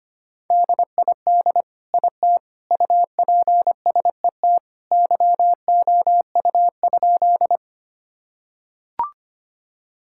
Morse code, heard as a keyed tone, about 25 wpm, 700 Hz. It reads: DID IT UPSET YOU?
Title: Sentences from Top 1000 Words - Review Part 5 - 25wpm